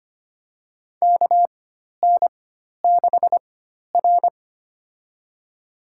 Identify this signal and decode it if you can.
Morse code, heard as a keyed tone, about 25 wpm, spaced as for 12 wpm, 690 Hz. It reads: KN6R
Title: USA Call Signs - 4 Characters - 4x Character Spacing - 25wpm